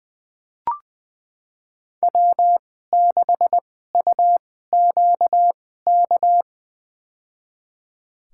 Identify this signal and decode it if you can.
Morse code, heard as a keyed tone, about 20 wpm, spaced as for 14 wpm, 700 Hz. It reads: W6UQK
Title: USA Call Signs - 5 Characters - 2x Character Spacing - 20wpm